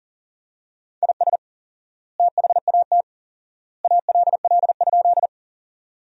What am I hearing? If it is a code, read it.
Morse code, heard as a keyed tone, about 40 wpm, 700 Hz. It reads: IS THAT ALL?
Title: 4x Word-Spacing - Sentences from Top 100 Words - 40wpm